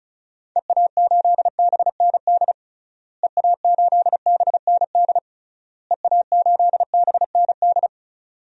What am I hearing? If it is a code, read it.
Morse code, heard as a keyed tone, about 36 wpm, 700 Hz. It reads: EA8BND EA8BND EA8BND
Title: DXCC-100 Call Signs - 6 Characters - Repeated 3x - 35wpm